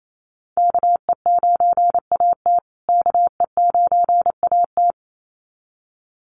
Morse code, heard as a keyed tone, about 28 wpm, 700 Hz. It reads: KE9AT KE9AT
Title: USA Call Signs Repeated 2x 28wpm